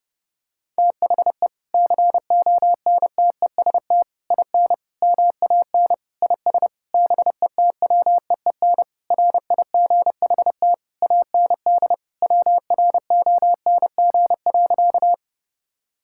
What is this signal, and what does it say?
Morse code, heard as a keyed tone, about 30 wpm, 700 Hz. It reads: THE CONTEST IN MAN IS BETWEEN RIGHT AND WRONG.
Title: Sentences from Top 1600 Words 30wpm